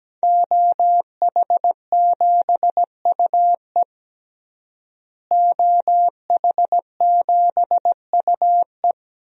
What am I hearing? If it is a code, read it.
Morse code, heard as a keyed tone, about 17 wpm, 700 Hz. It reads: OH7UE OH7UE
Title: DXCC-100 Call Signs - 5 Characters - Repeated 3x - 17wpm